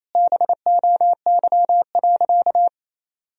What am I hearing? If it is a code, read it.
Morse code, heard as a keyed tone, about 28 wpm, 700 Hz. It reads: BOY.
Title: Sentences from Top 200 Words and Abbreviations 28wpm